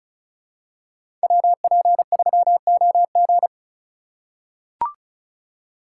Morse code, heard as a keyed tone, about 36 wpm, 700 Hz. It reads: WP3OG
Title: USA Call Signs - 5 Characters - 2x Character Spacing - 35wpm